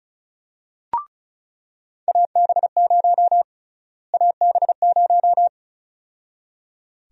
Morse code, heard as a keyed tone, about 36 wpm, 700 Hz. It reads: AB0 AB0
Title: USA Call Sign Prefixes - Repeated 2x - 35wpm